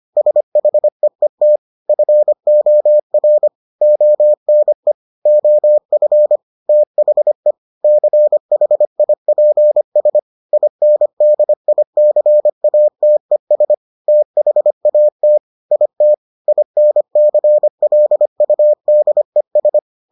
Morse code, heard as a keyed tone, about 25 wpm, 600 Hz. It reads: SHEET FOR ONE OF THE CHIPS INDICATES THAT IT INCLUDES